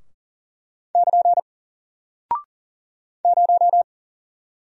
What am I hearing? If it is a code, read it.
Morse code, heard as a keyed tone, about 40 wpm, 700 Hz. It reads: <KN> 0
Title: Single Character (Letter:Number:Punctuation:Prosign) Alt 40wpm